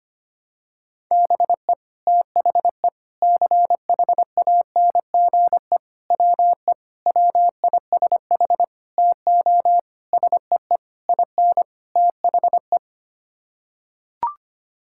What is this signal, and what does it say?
Morse code, heard as a keyed tone, about 25 wpm, 700 Hz. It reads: BE THE CHANGE WE WISH TO SEE IN THE
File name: Sets of 9 Words - 25wpm